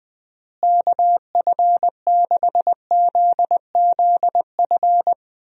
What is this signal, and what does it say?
Morse code, heard as a keyed tone, about 20 wpm, 700 Hz. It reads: KF6ZZF
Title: USA Call Signs - 6 Characters - Repeated 3x - 20wpm